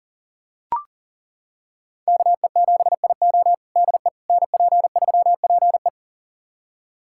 Morse code, heard as a keyed tone, about 37 wpm, 700 Hz. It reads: KE7IO DE NP3PE
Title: USA Callsign DE Callsign - 5 Characters - 40wpm